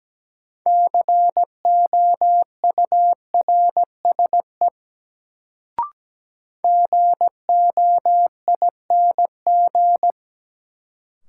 Morse code, heard as a keyed tone, about 17 wpm, 700 Hz. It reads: COURSE GOING
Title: Sentences from Top 500 Words - New Word Drill - 17wpm